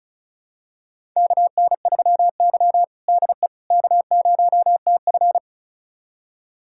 Morse code, heard as a keyed tone, about 34 wpm, 700 Hz. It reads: KN3Y DE K0TF
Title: USA Callsign DE Callsign - 4 Characters - 35wpm